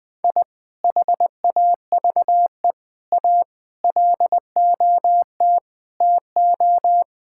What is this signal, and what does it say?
Morse code, heard as a keyed tone, about 20 wpm, 700 Hz. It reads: I HAVE A LOT TO
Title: Sentences with 2-Word Phrases - Spoken Beginning - 20wpm